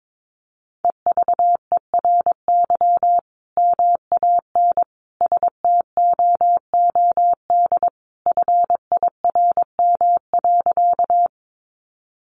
Morse code, heard as a keyed tone, about 22 wpm, 700 Hz. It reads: EVERY MAN STOOD FIRM.
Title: Sentences from Top 1400 Words 22wpm